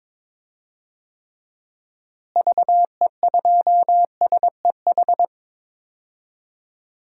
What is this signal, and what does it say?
Morse code, heard as a keyed tone, about 22 wpm, 700 Hz. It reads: VE2SEH
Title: DXCC-100 Call Signs - 6 Characters - Repeated 2x - 22wpm